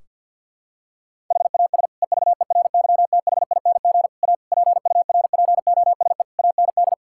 Morse code, heard as a keyed tone, about 50 wpm, 700 Hz. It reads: HAS EVERYTHING A PURPOSE AND
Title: Sentences from Top 1600 Words 50wpm